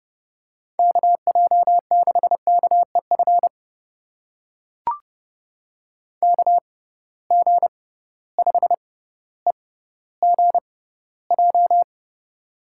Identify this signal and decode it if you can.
Morse code, heard as a keyed tone, about 30 wpm, spaced as for 15 wpm, 700 Hz. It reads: KJ6KEF KG5EGJ
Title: USA Call Signs - 6 Characters - 6x Character Spacing - 30wpm